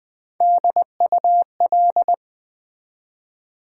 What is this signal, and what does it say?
Morse code, heard as a keyed tone, about 20 wpm, 700 Hz. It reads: DUL
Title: Build A Word - 1 Letter At A Time - 4-Letter Words - 20wpm